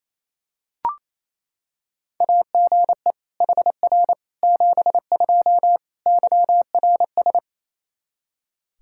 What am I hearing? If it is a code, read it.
Morse code, heard as a keyed tone, about 28 wpm, 700 Hz. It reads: AGE HR 72 YRS
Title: K1USN SST QSO Practice 28wpm